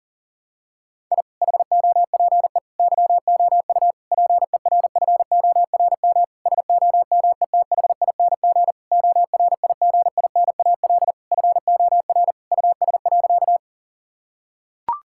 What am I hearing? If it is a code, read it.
Morse code, heard as a keyed tone, about 40 wpm, 700 Hz. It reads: I HOPE YOU PERFORM SOMETHING ORIGINAL FOR US.
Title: Sentences from Top 1000 Words - Review Part 6 - 40wpm